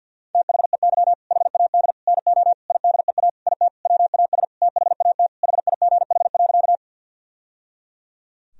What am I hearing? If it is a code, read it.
Morse code, heard as a keyed tone, about 50 wpm, 700 Hz. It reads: THEY HAD NO IDEA IT WAS THAT HIGH.
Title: Sentences from Top 1000 Words - Review Part 1 - 50wpm